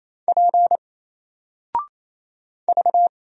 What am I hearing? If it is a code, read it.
Morse code, heard as a keyed tone, about 28 wpm, 700 Hz. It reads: P V
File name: Single-Letter-Number-28wpm